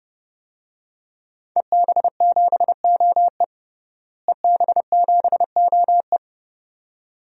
Morse code, heard as a keyed tone, about 32 wpm, 700 Hz. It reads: EB7OE EB7OE
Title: DXCC-100 Call Signs - 5 Characters - Repeated 2x - 30wpm